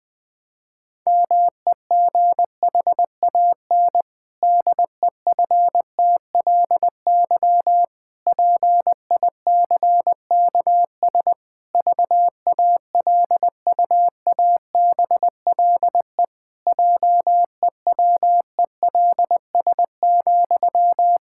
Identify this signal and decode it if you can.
Morse code, heard as a keyed tone, about 20 wpm, 700 Hz. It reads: MEGHAN DEFTLY PICKS VALUABLE JEWELS,
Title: Pangram Challenge v2 - 20wpm